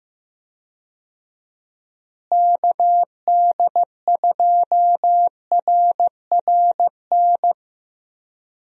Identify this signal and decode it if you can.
Morse code, heard as a keyed tone, about 16 wpm, 700 Hz. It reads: KD2RRN
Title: USA Call Signs - 6 Characters - Repeated 3x - 15wpm